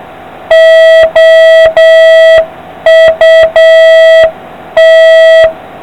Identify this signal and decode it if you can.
Morse code, heard as a keyed tone, about 7 wpm, 650 Hz, with noise. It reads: OUT